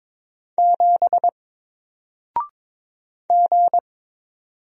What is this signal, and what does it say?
Morse code, heard as a keyed tone, about 22 wpm, 700 Hz. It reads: 7 G
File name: Single-Letter-Number-22wpm